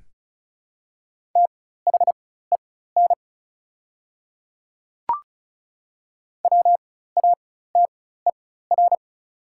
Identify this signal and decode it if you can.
Morse code, heard as a keyed tone, about 36 wpm, spaced as for 16 wpm, 700 Hz. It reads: THEN WATER
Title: 4x Character Spacing - Top 100 Words - 35wpm